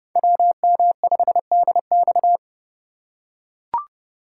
Morse code, heard as a keyed tone, about 30 wpm, 700 Hz. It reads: WM5DX
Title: K1USN SST QSO Practice - 30wpm